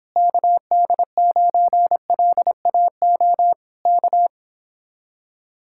Morse code, heard as a keyed tone, about 26 wpm, 700 Hz. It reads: KD9LAO K